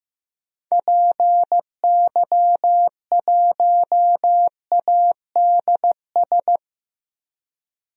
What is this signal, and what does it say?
Morse code, heard as a keyed tone, about 15 wpm, 700 Hz. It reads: PY1ADS
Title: DXCC-100 Call Signs - 6 Characters - 15wpm